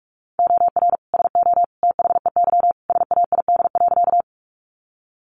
Morse code, heard as a keyed tone, about 44 wpm, 700 Hz. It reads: OR SO THEY SAID.
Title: Sentences from Top 100 Words 45wpm